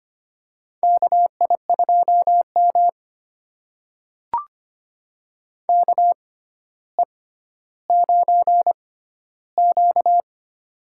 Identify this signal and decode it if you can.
Morse code, heard as a keyed tone, about 25 wpm, spaced as for 13 wpm, 700 Hz. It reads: KI2M KE9Q
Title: USA Call Signs - 4 Characters - 6x Character Spacing - 25wpm